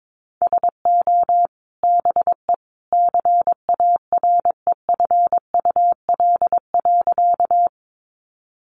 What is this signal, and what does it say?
Morse code, heard as a keyed tone, about 22 wpm, 700 Hz. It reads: SO BE CAREFUL.
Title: A Sound Of Thunder 22wpm